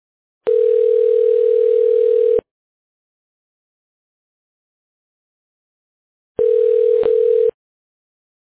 Ringback tone (North America), 440 + 480 Hz.